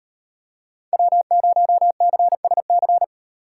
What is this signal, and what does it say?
Morse code, heard as a keyed tone, about 32 wpm, 700 Hz. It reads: W0CSC